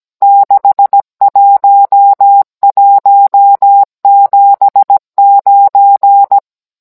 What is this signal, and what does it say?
Morse code, heard as a keyed tone, about 17 wpm, 800 Hz. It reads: 61179
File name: Cours-NR19-16mm